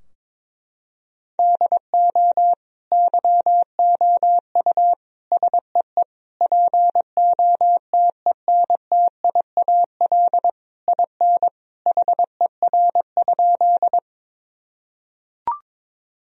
Morse code, heard as a keyed tone, about 22 wpm, 700 Hz. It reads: DO YOU SEE POTENTIAL IN HER?
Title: Sentences from Top 1000 Words - Review Part 4 - 22wpm